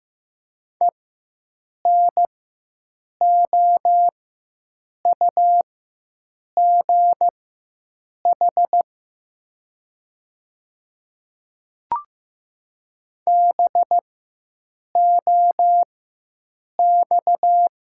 Morse code, heard as a keyed tone, about 15 wpm, spaced as for 7 wpm, 700 Hz. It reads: ENOUGH BOX
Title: 4x Character Spacing - Top 500 Words - 15wpm